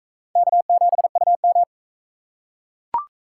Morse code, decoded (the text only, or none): K7UM